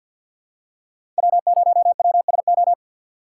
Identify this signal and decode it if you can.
Morse code, heard as a keyed tone, about 50 wpm, 700 Hz. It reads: W0WSO
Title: USA Call Signs - 5 Characters - 2x Character Spacing - 50wpm